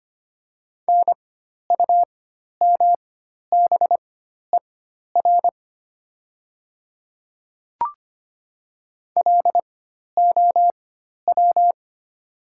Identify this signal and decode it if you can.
Morse code, heard as a keyed tone, about 25 wpm, spaced as for 12 wpm, 700 Hz. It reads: NUMBER LOW
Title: 4x Character Spacing - Top 300 Words - 25wpm